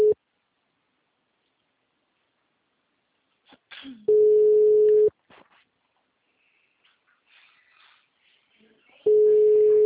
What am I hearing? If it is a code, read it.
Ringback tone (Europe, 425 Hz), with speech.
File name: out- -2006- - - .394